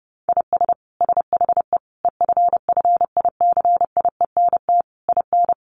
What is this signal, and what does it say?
Morse code, heard as a keyed tone, about 30 wpm, 700 Hz. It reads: IS SHE EFFICIENT IN
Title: Sentences from Top 1300 Words 30wpm